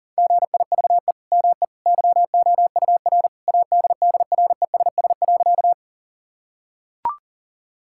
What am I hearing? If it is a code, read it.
Morse code, heard as a keyed tone, about 37 wpm, 700 Hz. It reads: GIVE ME YOUR ADDRESS.
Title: Sentences from Top 1000 Words - Review Part 6 - 40wpm